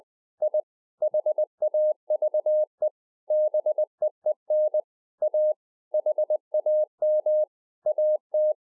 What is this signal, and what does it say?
Morse code, heard as a keyed tone, about 20 wpm, 610 Hz. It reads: I HAVE BEEN A HAM AT